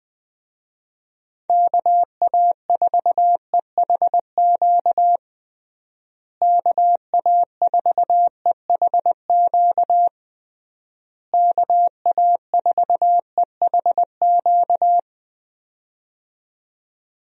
Morse code, heard as a keyed tone, about 20 wpm, 700 Hz. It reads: KA4EHQ KA4EHQ KA4EHQ
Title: USA Call Signs - 6 Characters - Repeated 3x - 20wpm